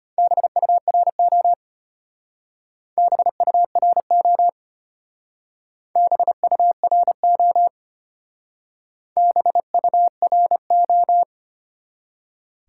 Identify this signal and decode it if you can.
Morse code, heard as a keyed tone, about 32 wpm, 700 Hz. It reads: BURO BURO BURO BURO